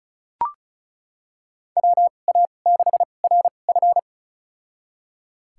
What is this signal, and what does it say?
Morse code, heard as a keyed tone, about 34 wpm, spaced as for 25 wpm, 690 Hz. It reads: WA6RF
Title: USA Call Signs - 5 Characters - 2x Character Spacing - 35wpm